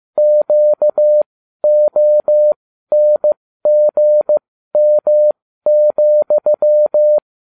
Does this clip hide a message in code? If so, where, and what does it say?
Yes: Morse code from 0.2 to 7.2 s, heard as a keyed tone, about 15 wpm, spaced as for 12 wpm, 600 Hz. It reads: QONGM,